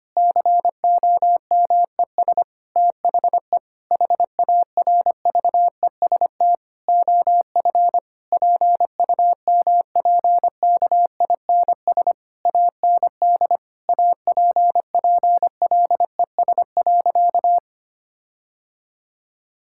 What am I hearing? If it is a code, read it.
Morse code, heard as a keyed tone, about 25 wpm, 700 Hz. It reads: COMES THE HARVEST OF PUMPKINS AND APPLES.
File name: practice / 25wpm / Difficult Sentences with Hints - 25wpm